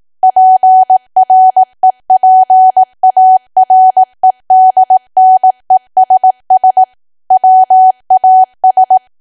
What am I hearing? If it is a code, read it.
Morse code, heard as a keyed tone, about 18 wpm, 750 Hz. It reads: PREPAREDNESS WAS
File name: Archive / 18 WPM / 18WPM